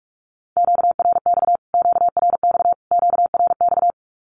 Morse code, heard as a keyed tone, about 44 wpm, 700 Hz. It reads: QRX QRX QRX
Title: Q-Codes Repeated 3x 45wpm